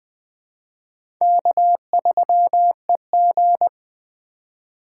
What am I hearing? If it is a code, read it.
Morse code, heard as a keyed tone, about 20 wpm, 700 Hz. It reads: K3EG
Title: Speed Racing - USA Call Signs 17wpm